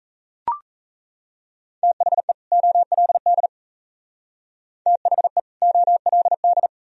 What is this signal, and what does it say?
Morse code, heard as a keyed tone, about 36 wpm, 690 Hz. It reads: THE OLD THE OLD